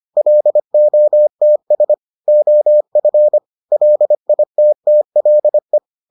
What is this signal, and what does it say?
Morse code, heard as a keyed tone, about 25 wpm, 600 Hz. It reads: LOTS OF LITTLE